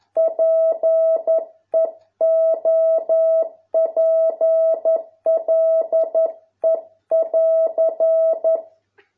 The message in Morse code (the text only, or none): PEOPLE<AR>